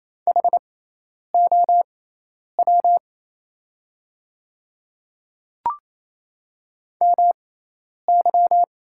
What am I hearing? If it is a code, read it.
Morse code, heard as a keyed tone, about 28 wpm, spaced as for 10 wpm, 700 Hz. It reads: HOW MY
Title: 6x Character Spacing - Top 100 Words - 28wpm